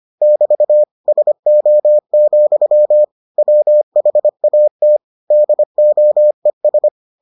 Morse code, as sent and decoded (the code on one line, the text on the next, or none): -...- / ... --- --..-- / .-- .... .- - / -.. --- . ...
<BT> SO, WHAT DOES